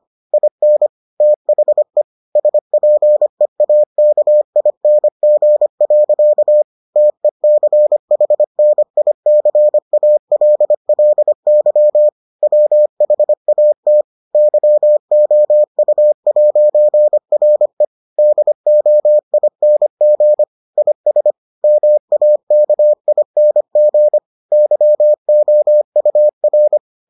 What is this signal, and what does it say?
Morse code, heard as a keyed tone, about 25 wpm, 600 Hz. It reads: IN THE SPEAKING. TECHNICALLY WHAT YOU'RE DOING IS MAKING YOUR